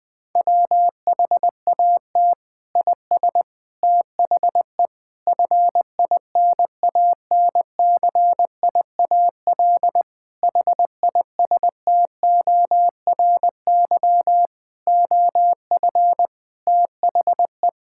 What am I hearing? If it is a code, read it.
Morse code, heard as a keyed tone, about 20 wpm, 700 Hz. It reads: WHAT IS THE FINANCIAL HISTORY OF THE
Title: Sentences from Top 1000 Words - Review Part 5 - 20wpm